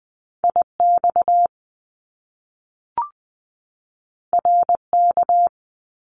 Morse code, heard as a keyed tone, about 20 wpm, 700 Hz. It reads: I X R K
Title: Top Two Letter Combinations in Words 20wpm